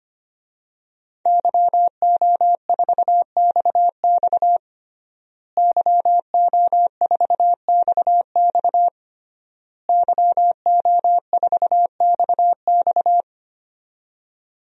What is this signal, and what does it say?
Morse code, heard as a keyed tone, about 25 wpm, 700 Hz. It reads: YO4XX YO4XX YO4XX
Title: DXCC-100 Call Signs - 5 Characters - Repeated 3x - 25wpm